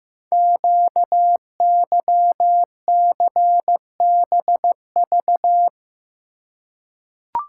Morse code, heard as a keyed tone, about 15 wpm, 700 Hz. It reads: QYCBV
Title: Code Groups - 5 Letters - 15wpm